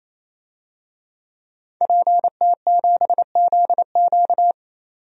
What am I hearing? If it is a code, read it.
Morse code, heard as a keyed tone, about 28 wpm, 700 Hz. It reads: PT7ZQ
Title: DXCC-100 Call Signs - 5 Characters - 28wpm